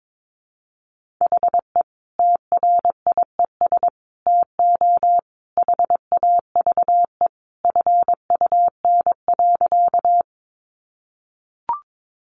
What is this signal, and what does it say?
Morse code, heard as a keyed tone, about 22 wpm, 700 Hz. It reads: HE TRIES TO HAVE FUN.
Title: Easy Phrases 22wpm